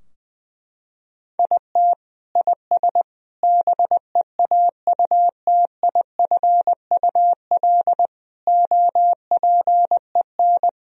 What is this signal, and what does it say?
Morse code, heard as a keyed tone, about 21 wpm, 700 Hz. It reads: IT IS BEAUTIFUL OPEN
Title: Sentences from Top 1000 Words - Review Part 2 - 20wpm